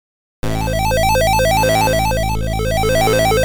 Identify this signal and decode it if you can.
There is no code.